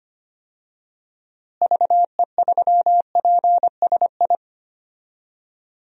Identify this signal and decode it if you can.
Morse code, heard as a keyed tone, about 25 wpm, 700 Hz. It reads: VE3PSI
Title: DXCC-100 Call Signs - 6 Characters - Repeated 2x - 25wpm